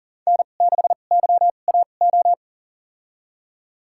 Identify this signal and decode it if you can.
Morse code, heard as a keyed tone, about 40 wpm, spaced as for 29 wpm, 700 Hz. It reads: N6YAO
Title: USA Call Signs - 5 Characters - 2x Character Spacing - 40wpm